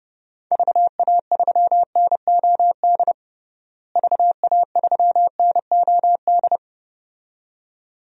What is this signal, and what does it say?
Morse code, heard as a keyed tone, about 30 wpm, 700 Hz. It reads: VA3NOD VA3NOD
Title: DXCC-100 Call Signs - 6 Characters - Repeated 2x - 30wpm